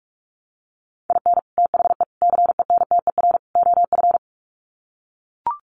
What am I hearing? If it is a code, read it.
Morse code, heard as a keyed tone, about 44 wpm, 700 Hz. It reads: IN THE CENTER OF
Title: Sets of 4 Words 45wpm